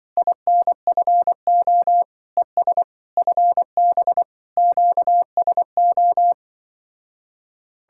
Morse code, heard as a keyed tone, about 24 wpm, 700 Hz. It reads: INFO ES FB QSO
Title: Speed-Racing Common QSO Phrases - 20wpm